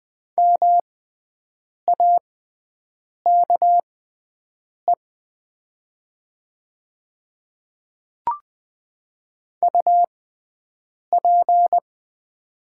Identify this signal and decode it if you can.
Morse code, heard as a keyed tone, about 20 wpm, spaced as for 7 wpm, 700 Hz. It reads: MAKE UP
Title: 6x Character Spacing - Top 500 Words - 20wpm